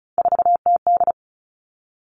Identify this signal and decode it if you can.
Morse code, heard as a keyed tone, about 36 wpm, 700 Hz. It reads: 4TD
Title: Three Challenging Letter-Number Combinations 35wpm